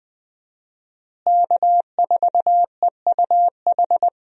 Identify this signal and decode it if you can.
Morse code, heard as a keyed tone, about 20 wpm, 700 Hz. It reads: K4EUH
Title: USA Call Signs - 5 Characters - Repeated 3x - 20wpm